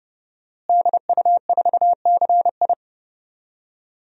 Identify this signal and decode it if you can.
Morse code, heard as a keyed tone, about 30 wpm, 700 Hz. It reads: DU4CI
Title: DXCC-100 Call Signs - 5 Characters - Repeated 2x - 30wpm